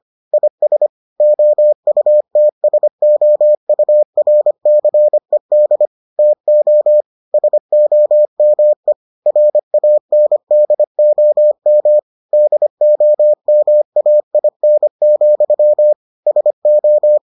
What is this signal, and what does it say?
Morse code, heard as a keyed tone, about 25 wpm, 600 Hz. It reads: IS OUTSOURCED TO SOME RANDOM DOMAIN, SO